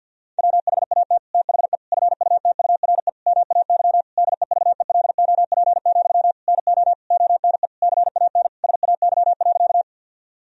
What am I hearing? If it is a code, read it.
Morse code, heard as a keyed tone, about 52 wpm, 700 Hz. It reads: WHAT THE FUTURE MAY DEVELOP, NO ONE CAN SAY.
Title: Sentences from Top 1000 Words - Review Part 2 - 50wpm